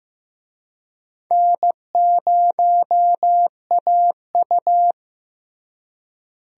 Morse code, heard as a keyed tone, about 15 wpm, 700 Hz. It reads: N0AU
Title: USA Call Signs - 4 Characters - Repeated 3x - 15wpm